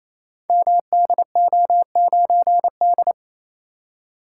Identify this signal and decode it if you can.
Morse code, heard as a keyed tone, about 27 wpm, 700 Hz. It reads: MDO9D
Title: Code Groups - 5 Letter:Numbers - 28wpm